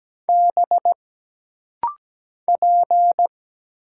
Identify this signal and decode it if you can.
Morse code, heard as a keyed tone, about 17 wpm, 700 Hz. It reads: B P
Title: Single Letter-Number - Rapid-Fire 17wpm